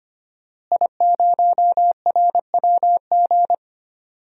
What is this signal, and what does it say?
Morse code, heard as a keyed tone, about 25 wpm, 700 Hz. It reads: I0RWG
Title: DXCC-100 Call Signs - 5 Characters - Repeated 2x - 25wpm